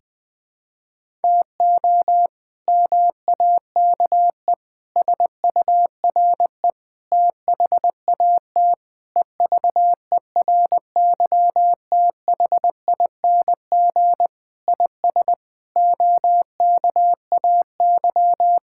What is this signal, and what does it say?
Morse code, heard as a keyed tone, about 20 wpm, 700 Hz. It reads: TO MAKE SURE THAT EVERYTHING IS OKAY
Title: Sentences with 4-Word Phrases - 20wpm